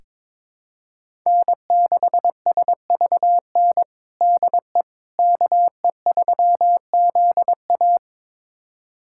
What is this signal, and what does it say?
Morse code, heard as a keyed tone, about 22 wpm, 700 Hz. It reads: N6SVN DE KE3ZA
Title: USA Callsign DE Callsign - 5 Characters - 22wpm